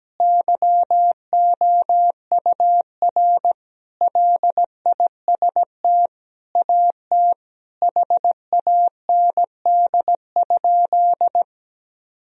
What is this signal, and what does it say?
Morse code, heard as a keyed tone, about 17 wpm, 700 Hz. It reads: YOUR LIST AT HAND?
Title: Sentences from Top 1000 Words - Review Part 2 - 17wpm